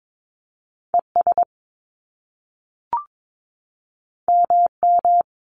Morse code, heard as a keyed tone, about 22 wpm, 700 Hz. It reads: ES MM
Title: Top Two Letter Combinations in Words 22wpm